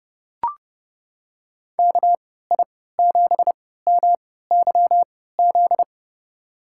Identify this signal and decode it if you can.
Morse code, heard as a keyed tone, about 30 wpm, spaced as for 17 wpm, 700 Hz. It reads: KI7MYZ
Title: USA Call Signs - 3x Character Spacing - 30wpm